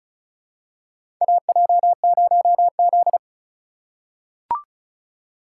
Morse code, heard as a keyed tone, about 34 wpm, 700 Hz. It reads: AJ0Z
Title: USA Call Signs - 4 Characters - 3x Character Spacing - 35wpm